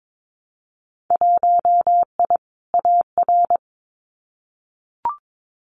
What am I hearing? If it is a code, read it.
Morse code, heard as a keyed tone, about 22 wpm, 700 Hz. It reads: 1I AR
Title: Winter Field Day Exchanges Encore 22wpm